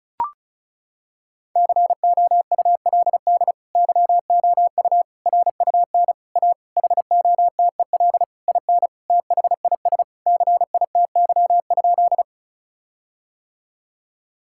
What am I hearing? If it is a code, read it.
Morse code, heard as a keyed tone, about 36 wpm, 700 Hz. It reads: COULD YOU RUN A HOTEL IN THIS CITY?
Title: Sentences from Top 1600 Words 35wpm